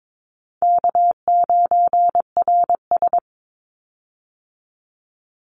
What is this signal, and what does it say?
Morse code, heard as a keyed tone, about 22 wpm, 700 Hz. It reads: K9RS
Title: Call Signs - 22wpm